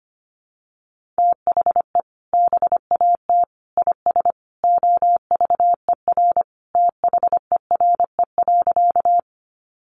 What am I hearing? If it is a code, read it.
Morse code, heard as a keyed tone, about 25 wpm, 700 Hz. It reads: THE BAT IS OVER THERE.
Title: Sentences from Top 1500 Words 25wpm